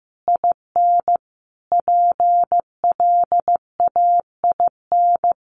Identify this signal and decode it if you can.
Morse code, heard as a keyed tone, about 15 wpm, 700 Hz. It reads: IN PLAIN
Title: Sentences from Top 1400 Words 15wpm